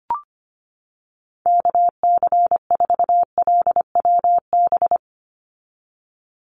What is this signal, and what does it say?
Morse code, heard as a keyed tone, about 25 wpm, 690 Hz. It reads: KC4LWB
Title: USA Call Signs 25wpm